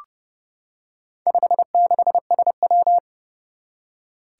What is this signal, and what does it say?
Morse code, heard as a keyed tone, about 30 wpm, 690 Hz. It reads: 56SW
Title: Four Challenging Letter-Number Combinations 30wpm